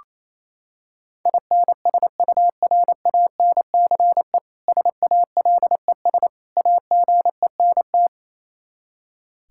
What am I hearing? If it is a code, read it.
Morse code, heard as a keyed tone, about 28 wpm, 700 Hz. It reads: INSURANCE SALES AGENT
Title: Practice QSO - Exchange 3 - More Info - 28wpm